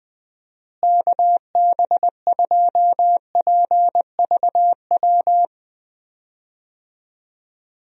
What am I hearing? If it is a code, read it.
Morse code, heard as a keyed tone, about 21 wpm, 700 Hz. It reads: KB2PVW
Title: USA Call Signs - 6 Characters - Repeated 2x - 20wpm